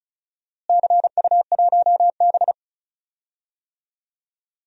Morse code, heard as a keyed tone, about 36 wpm, 700 Hz. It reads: CU1B
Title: DXCC-100 Call Signs - 4 Characters - Repeated 3x - 35wpm